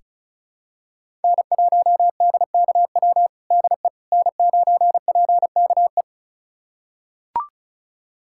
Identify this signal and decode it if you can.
Morse code, heard as a keyed tone, about 36 wpm, 700 Hz. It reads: N1DKW DE N9PKE
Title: USA Callsign DE Callsign - 5 Characters - 35wpm